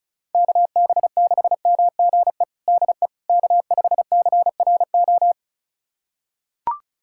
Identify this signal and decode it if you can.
Morse code, heard as a keyed tone, about 36 wpm, 700 Hz. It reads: KB6MGE DE K5CRO
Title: USA Callsign DE Callsign 35wpm